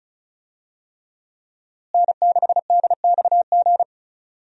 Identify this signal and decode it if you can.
Morse code, heard as a keyed tone, about 36 wpm, 700 Hz. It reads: N6DXG
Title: USA Call Signs - 5 Characters - Repeated 3x - 35wpm